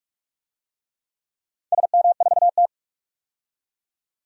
Morse code, heard as a keyed tone, about 46 wpm, 690 Hz. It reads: SM4T
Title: DXCC-100 Call Signs - 4 Characters - 45wpm